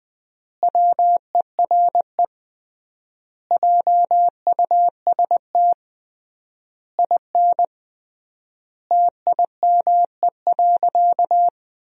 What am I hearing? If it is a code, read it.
Morse code, heard as a keyed tone, about 20 wpm, 700 Hz. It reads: WERE JUST IN TIME.
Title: 3x Word-Spacing - Sentences from Top 300 Words - 20wpm